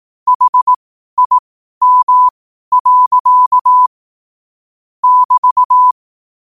Morse code, heard as a keyed tone, about 18 wpm, spaced as for 12 wpm, 1000 Hz. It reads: HIM. <BT>
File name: qotdfiles / 12wpm